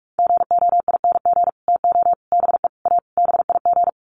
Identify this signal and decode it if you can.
Morse code, heard as a keyed tone, about 44 wpm, 700 Hz. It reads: GOING TO BE A BIG
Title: Sets of 5 Words Encore2 45wpm